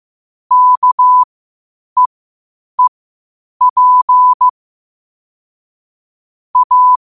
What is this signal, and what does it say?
Morse code, heard as a keyed tone, about 15 wpm, spaced as for 8 wpm, 1000 Hz. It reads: KEEP A
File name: qotdfiles / 8wpm